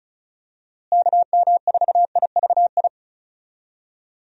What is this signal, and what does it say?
Morse code, heard as a keyed tone, about 36 wpm, 700 Hz. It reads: KM4IVI
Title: USA Call Signs - 6 Characters - Repeated 2x - 35wpm